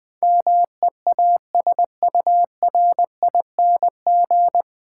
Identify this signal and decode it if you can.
Morse code, heard as a keyed tone, about 20 wpm, 700 Hz. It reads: MEASURING
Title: Difficult Words - Speed-Racing with Hints - 20wpm